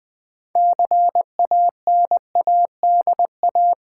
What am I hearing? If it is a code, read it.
Morse code, heard as a keyed tone, about 20 wpm, 700 Hz. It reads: CANADA
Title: Sets of 5 Words - New Word Drill - 20wpm